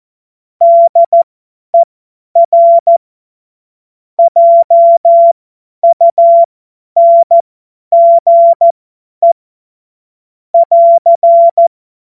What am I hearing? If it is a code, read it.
Morse code, heard as a keyed tone, about 14 wpm, spaced as for 10 wpm, 680 Hz. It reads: DER JUNGE <AR>
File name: Morsekurs / Witze / Witz 04. Oktober 2022 50 F70